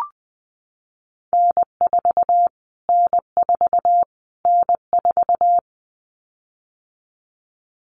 Morse code, heard as a keyed tone, about 19 wpm, 700 Hz. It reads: N4 N4 N4
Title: Callsign Prefixes for Top 25 DXCC - Repeated 3x - 20wpm